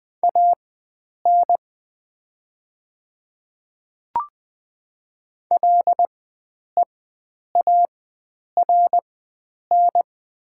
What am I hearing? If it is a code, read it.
Morse code, heard as a keyed tone, about 20 wpm, spaced as for 9 wpm, 690 Hz. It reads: AN LEARN
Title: 4x Character Spacing - Top 300 Words - 20wpm